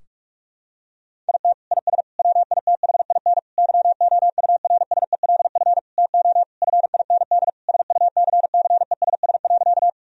Morse code, heard as a keyed tone, about 46 wpm, 700 Hz. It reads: IT IS WITHIN YOURSELF TO FIND SUCCESS.
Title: Sentences from Top 1000 Words - Review Part 4 - 45wpm